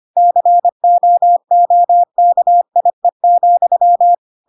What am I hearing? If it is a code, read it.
Morse code, heard as a keyed tone, about 25 wpm, 700 Hz. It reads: COOKIE,
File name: Pinocchio / Pinocchio ch5 25 25 ik1hge